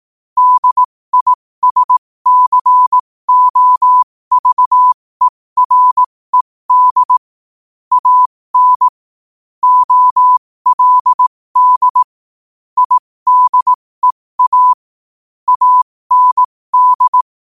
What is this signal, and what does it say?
Morse code, heard as a keyed tone, about 18 wpm, spaced as for 15 wpm, 1000 Hz. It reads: DISCOVERED AN OLD IDEA AND
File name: qotdfiles / 15wpm